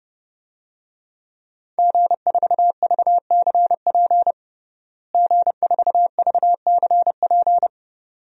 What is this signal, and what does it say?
Morse code, heard as a keyed tone, about 30 wpm, 700 Hz. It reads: G4VCP G4VCP
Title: DXCC-100 Call Signs - 5 Characters - Repeated 2x - 30wpm